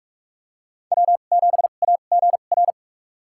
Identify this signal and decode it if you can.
Morse code, heard as a keyed tone, about 46 wpm, spaced as for 33 wpm, 700 Hz. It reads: W7AGR
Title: USA Call Signs - 5 Characters - 2x Character Spacing - 45wpm